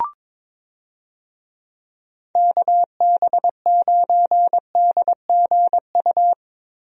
Morse code, heard as a keyed tone, about 22 wpm, 700 Hz. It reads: KB9DGU
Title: USA Call Signs - 6 Characters - Repeated 3x - 22wpm